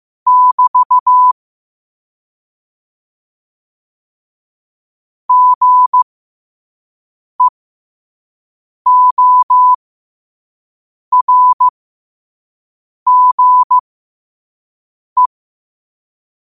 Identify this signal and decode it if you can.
Morse code, heard as a keyed tone, about 15 wpm, spaced as for 5 wpm, 1000 Hz. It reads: <BT> GEORGE
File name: qotdfiles / 5wpm